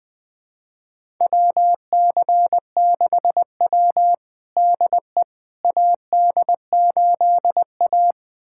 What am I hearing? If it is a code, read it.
Morse code, heard as a keyed tone, about 20 wpm, 700 Hz. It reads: WC6W DE AD8A
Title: USA Callsign DE Callsign - 4 Characters - 20wpm